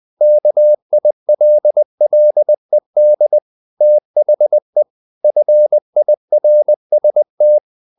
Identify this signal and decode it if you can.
Morse code, heard as a keyed tone, about 20 wpm, 600 Hz. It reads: KILLED THE FIRST
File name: the old man and the sea part 2 20wpm